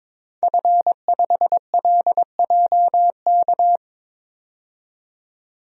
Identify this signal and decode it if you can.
Morse code, heard as a keyed tone, about 22 wpm, 700 Hz. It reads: F5LJK
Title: DXCC-100 Call Signs - 5 Characters - 22wpm